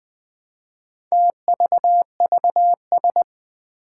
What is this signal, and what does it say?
Morse code, heard as a keyed tone, about 20 wpm, 700 Hz. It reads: TVVS
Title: Four Challenging Letter-Number Combinations 20wpm